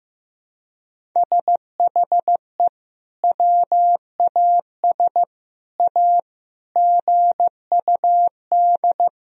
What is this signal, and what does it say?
Morse code, heard as a keyed tone, about 15 wpm, 700 Hz. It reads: SHE WAS A GUD
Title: Sentences from Top 400 Words and Abbreviations 15wpm